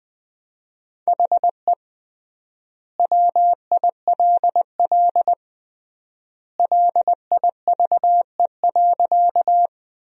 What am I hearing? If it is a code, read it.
Morse code, heard as a keyed tone, about 20 wpm, 700 Hz. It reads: HE WILL LIVE.
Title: 3x Word-Spacing - Sentences from Top 200 Words - 20wpm